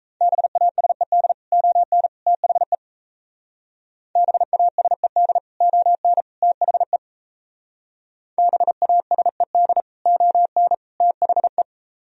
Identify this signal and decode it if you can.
Morse code, heard as a keyed tone, about 35 wpm, 700 Hz. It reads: BASED ON THE BASED ON THE BASED ON THE
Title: Speed-Racing - Sets of 3 Words 28wpm